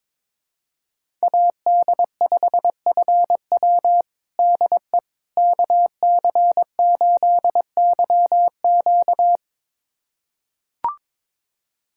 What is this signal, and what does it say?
Morse code, heard as a keyed tone, about 22 wpm, 700 Hz. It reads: AD5FW DE KC8YQ
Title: USA Callsign DE Callsign - 5 Characters - 22wpm